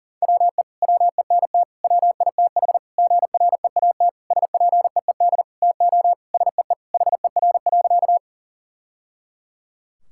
Morse code, heard as a keyed tone, about 40 wpm, 700 Hz. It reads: WE WENT WITH GREAT SPEED TO SEE HER.
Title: Sentences from Top 1000 Words - Review Part 5 - 40wpm